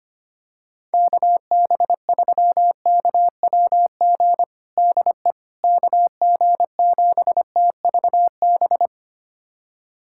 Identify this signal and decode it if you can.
Morse code, heard as a keyed tone, about 25 wpm, 700 Hz. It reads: KB3KWG DE KG7TVB
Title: USA Callsign DE Callsign 25wpm